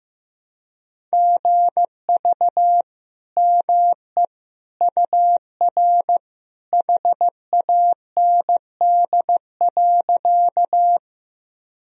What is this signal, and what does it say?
Morse code, heard as a keyed tone, about 15 wpm, 700 Hz. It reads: GV ME UR HAND.
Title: Sentences from Top 200 Words and Abbreviations 15wpm